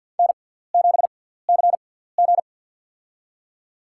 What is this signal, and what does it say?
Morse code, heard as a keyed tone, about 49 wpm, spaced as for 17 wpm, 700 Hz. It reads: N7CG
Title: USA Call Signs - 4 Characters - 6x Character Spacing - 50wpm